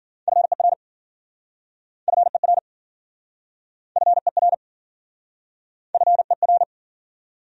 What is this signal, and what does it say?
Morse code, heard as a keyed tone, about 52 wpm, 700 Hz. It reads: FER FER FER FER